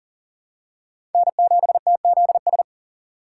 Morse code, heard as a keyed tone, about 37 wpm, 700 Hz. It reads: N7TZS
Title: USA Call Signs - 5 Characters - 6x Character Spacing - 40wpm